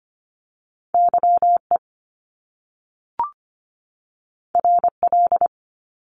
Morse code, heard as a keyed tone, about 25 wpm, 700 Hz. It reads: YE RL
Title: Top Two Letter Combinations in Words 25wpm